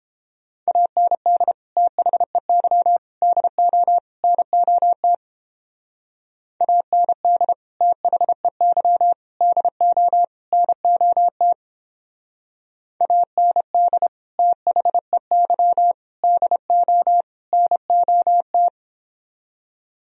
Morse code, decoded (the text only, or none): AND THEY DO NOT AND THEY DO NOT AND THEY DO NOT